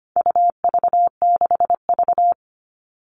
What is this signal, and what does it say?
Morse code, heard as a keyed tone, about 25 wpm, 700 Hz. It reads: UV6V
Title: All Cut Numbers - 4-Digit 25wpm